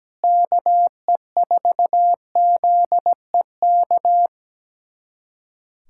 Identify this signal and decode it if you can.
Morse code, heard as a keyed tone, about 17 wpm, 700 Hz. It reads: KE4ZEK
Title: USA Callsign DE Callsign - 6 Characters - 17wpm